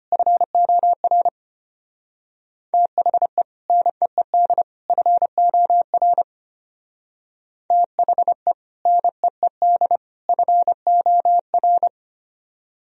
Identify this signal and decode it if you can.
Morse code, heard as a keyed tone, about 32 wpm, 700 Hz. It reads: FOR THE NEED FOR THE NEED FOR